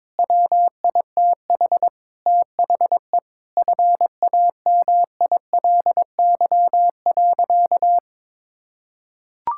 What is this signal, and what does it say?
Morse code, heard as a keyed tone, about 22 wpm, 700 Hz. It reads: WITH THE FAMILY.
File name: Sentences with 7-Word Phrases - 22wpm